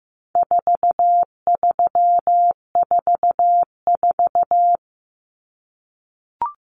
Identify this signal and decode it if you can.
Morse code, heard as a keyed tone, about 14 wpm, 700 Hz. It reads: 4344
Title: Common Cut Numbers - 4-Digit 15wpm